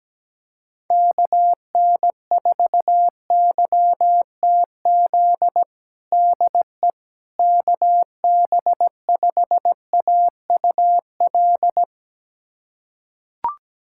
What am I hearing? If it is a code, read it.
Morse code, heard as a keyed tone, about 17 wpm, 700 Hz. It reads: KN4YTZ DE KB5AUL
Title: USA Callsign DE Callsign - 6 Characters - 17wpm